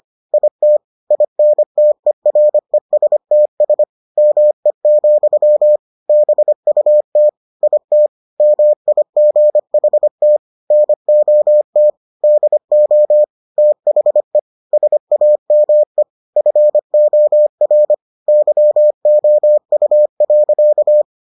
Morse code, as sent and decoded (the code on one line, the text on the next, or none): .. - / .. -. - . .-. ... - ... / -- . --..-- / -... ..- - / .. - / -- .. --. .... - / -. --- - / -.. --- / - .... / ... .- -- . / ..-. --- .-. / -.-- --- ..- .-.-.-
IT INTERESTS ME, BUT IT MIGHT NOT DO THE SAME FOR YOU.